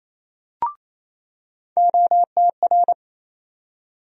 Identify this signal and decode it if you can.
Morse code, heard as a keyed tone, about 28 wpm, 700 Hz. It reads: OTR
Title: USA Call Sign Postfixes - 3 Letters 28wpm